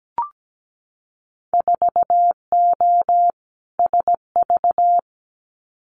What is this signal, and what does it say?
Morse code, heard as a keyed tone, about 17 wpm, 700 Hz. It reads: 4O SV
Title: Winter Field Day Exchanges 17wpm